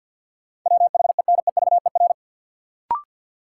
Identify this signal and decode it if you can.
Morse code, heard as a keyed tone, about 52 wpm, 700 Hz. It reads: WHENEVER
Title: Sentences from Top 1200 Words - New Word Drill - 50wpm